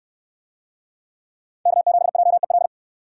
Morse code, heard as a keyed tone, about 69 wpm, 700 Hz. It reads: K7JEL